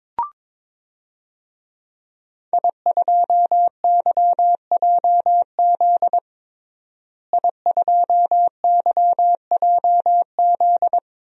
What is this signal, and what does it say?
Morse code, heard as a keyed tone, about 22 wpm, 700 Hz. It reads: I2YJZ I2YJZ
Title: DXCC-100 Call Signs - 5 Characters - Repeated 3x - 22wpm